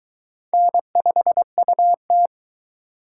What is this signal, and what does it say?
Morse code, heard as a keyed tone, about 21 wpm, 700 Hz. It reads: N5UT